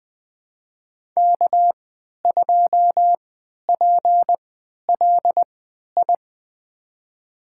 Morse code, heard as a keyed tone, about 20 wpm, spaced as for 11 wpm, 700 Hz. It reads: K2PLI
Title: USA Call Signs - 3x Character Spacing - 20wpm